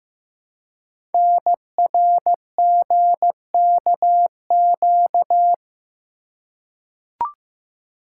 Morse code, heard as a keyed tone, about 15 wpm, 700 Hz. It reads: NRGKQ
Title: Code Groups - 5 Letters - 15wpm